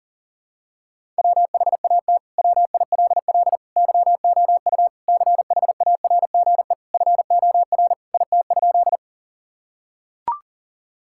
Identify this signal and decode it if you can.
Morse code, heard as a keyed tone, about 40 wpm, 700 Hz. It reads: WHAT WILL YOU CHARGE FOR IT?
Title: Sentences from Top 1000 Words - Review Part 6 - 40wpm